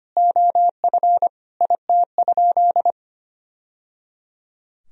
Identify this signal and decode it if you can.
Morse code, heard as a keyed tone, about 25 wpm, 700 Hz. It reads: OF IT?
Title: Sentences from Top 1000 Words - Review Part 3 - 25wpm